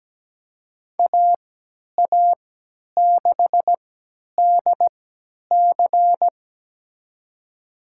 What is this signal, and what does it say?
Morse code, heard as a keyed tone, about 17 wpm, spaced as for 10 wpm, 700 Hz. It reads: AA6DC
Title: USA Call Signs - 5 Characters - 3x Character Spacing - 17wpm